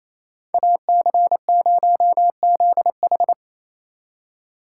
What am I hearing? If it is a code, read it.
Morse code, heard as a keyed tone, about 34 wpm, 700 Hz. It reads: AC0ZH
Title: Speed Racing - USA 2x2 Call Signs 28wpm